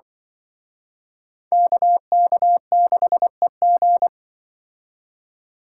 Morse code, heard as a keyed tone, about 25 wpm, 700 Hz. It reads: KK6EG
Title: Speed Racing - USA 2x2 Call Signs 20wpm